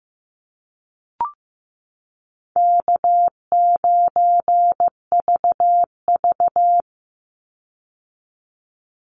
Morse code, heard as a keyed tone, about 15 wpm, 700 Hz. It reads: K9VV
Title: Easy Call Signs - 15wpm